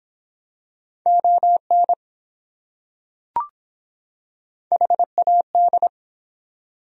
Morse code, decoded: ON HAD

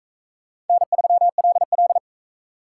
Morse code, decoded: N3LL

